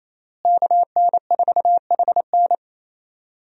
Morse code, heard as a keyed tone, about 28 wpm, 700 Hz. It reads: KN4HN
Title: USA Call Signs - 5 Characters - Repeated 3x - 28wpm